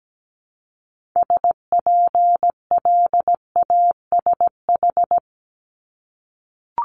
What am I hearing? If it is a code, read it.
Morse code, heard as a keyed tone, about 17 wpm, 700 Hz. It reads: SPLASH
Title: Multi-letter Phonemes Repeated 3x 17wpm